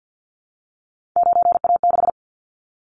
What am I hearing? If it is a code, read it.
Morse code, heard as a keyed tone, about 48 wpm, 700 Hz. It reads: 9A6
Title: Callsign Prefixes for Top 100 DXCC - 50wpm